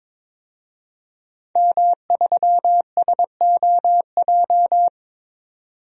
Morse code, heard as a keyed tone, about 22 wpm, 700 Hz. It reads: M3SOJ
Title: DXCC-100 Call Signs - 5 Characters - 22wpm